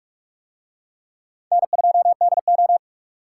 Morse code, heard as a keyed tone, about 42 wpm, 700 Hz. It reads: N2DO